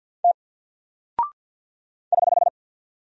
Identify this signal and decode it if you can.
Morse code, heard as a keyed tone, about 48 wpm, 700 Hz. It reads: T <HH>
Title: Single Character (Letter:Number:Punctuation:Prosign) Alt 50wpm